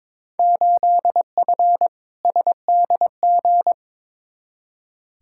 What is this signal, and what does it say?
Morse code, heard as a keyed tone, about 22 wpm, 700 Hz. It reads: 8F SDG
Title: Field Day Exchanges Encore - 22wpm